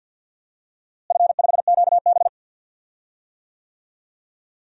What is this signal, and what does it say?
Morse code, heard as a keyed tone, about 52 wpm, 700 Hz. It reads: F5QB
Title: DXCC-100 Call Signs - 4 Characters - Repeated 3x - 50wpm